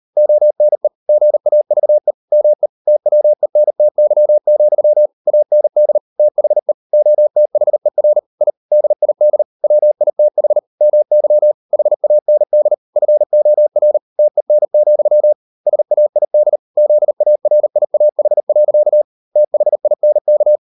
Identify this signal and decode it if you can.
Morse code, heard as a keyed tone, about 38 wpm, 600 Hz. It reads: ONE GAVE ME TWENTY, AND THE OTHER I DID WITH MY HAND FOR TEN, SAID ZARIAH. THINK